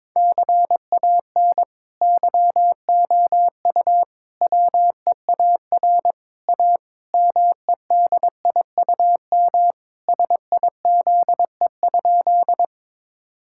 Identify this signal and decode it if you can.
Morse code, heard as a keyed tone, about 22 wpm, 700 Hz. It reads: CAN YOU WEAR A MEDIUM SIZE?
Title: Sentences from Top 1000 Words - Review Part 6 - 22wpm